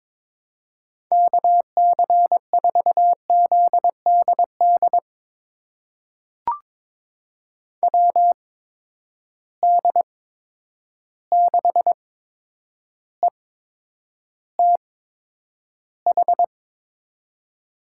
Morse code, heard as a keyed tone, about 21 wpm, spaced as for 9 wpm, 700 Hz. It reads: KC4ZDD WD6ETH
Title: USA Call Signs - 6 Characters - 8x Character Spacing - 22wpm